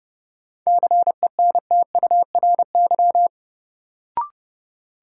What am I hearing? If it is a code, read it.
Morse code, heard as a keyed tone, about 31 wpm, 700 Hz. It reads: CENTURY
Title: Sets of 5 Words - New Word Drill - 30wpm